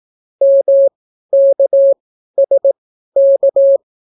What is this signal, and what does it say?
Morse code, heard as a keyed tone, about 18 wpm, spaced as for 12 wpm, 550 Hz. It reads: MKSK